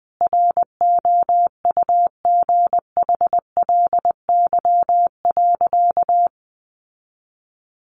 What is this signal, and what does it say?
Morse code, heard as a keyed tone, about 19 wpm, 700 Hz. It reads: ROUGHLY.
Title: Sentences from Top 1500 Words 20wpm